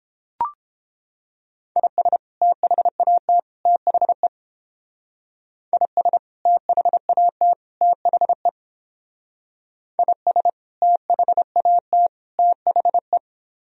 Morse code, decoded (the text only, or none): IS THAT THE IS THAT THE IS THAT THE